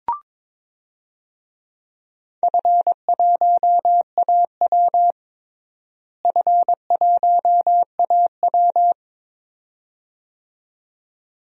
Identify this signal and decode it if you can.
Morse code, heard as a keyed tone, about 21 wpm, 700 Hz. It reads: F1AW F1AW
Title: DXCC-100 Call Signs - 4 Characters - Repeated 2x - 22wpm